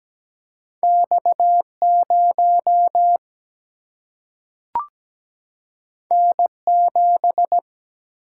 Morse code, heard as a keyed tone, about 17 wpm, 700 Hz. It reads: X 0 N 7
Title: Single-Letter Single-Number 17wpm